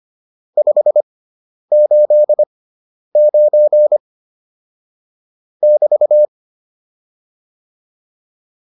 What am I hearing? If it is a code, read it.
Morse code, heard as a keyed tone, about 25 wpm, spaced as for 10 wpm, 600 Hz. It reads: 589 <BT>
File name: QSO302 10